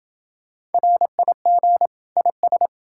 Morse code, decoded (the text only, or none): RIG IS